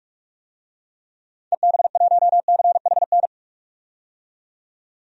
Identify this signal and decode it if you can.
Morse code, heard as a keyed tone, about 46 wpm, 700 Hz. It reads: EB1CHN
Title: DXCC-100 Call Signs - 6 Characters - 45wpm